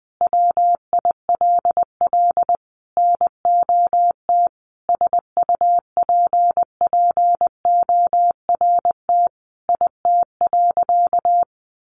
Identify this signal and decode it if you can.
Morse code, heard as a keyed tone, about 20 wpm, 700 Hz. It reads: WILL NOT SUPPORT IT.
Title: Sentences from Top 1300 Words 20wpm